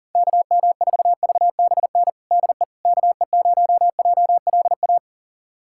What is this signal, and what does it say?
Morse code, heard as a keyed tone, about 40 wpm, 700 Hz. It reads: KM4VBN DE KE0JLA
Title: USA Callsign DE Callsign - 6 Characters - 40wpm